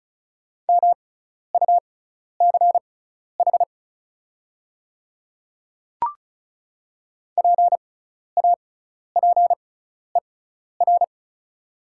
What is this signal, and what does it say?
Morse code, heard as a keyed tone, about 36 wpm, spaced as for 12 wpm, 690 Hz. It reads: MUCH PAPER
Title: 6x Character Spacing - Top 500 Words - 35wpm